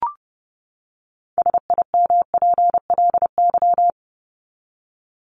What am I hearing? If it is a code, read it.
Morse code, heard as a keyed tone, about 30 wpm, 700 Hz. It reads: SIMPLY
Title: LY Suffix - Top 100 Words with Base Words 30wpm